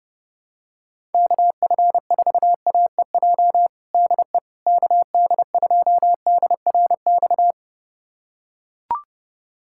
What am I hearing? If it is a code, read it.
Morse code, heard as a keyed tone, about 32 wpm, 700 Hz. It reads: KF4AEJ DE KD2DRX
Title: USA Callsign DE Callsign - 6 Characters - 30wpm